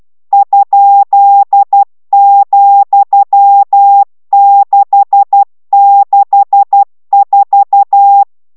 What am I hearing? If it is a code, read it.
Morse code, heard as a keyed tone, about 12 wpm, 800 Hz. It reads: ?,664